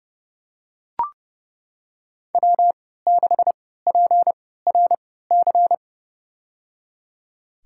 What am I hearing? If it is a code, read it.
Morse code, heard as a keyed tone, about 30 wpm, spaced as for 17 wpm, 700 Hz. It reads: W6PRC
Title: USA Call Signs - 5 Characters - 3x Character Spacing - 30wpm